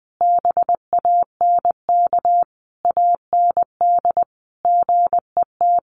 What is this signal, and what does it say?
Morse code, heard as a keyed tone, about 20 wpm, 700 Hz. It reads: BANK AND GET
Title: Sentences from Top 1500 Words 20wpm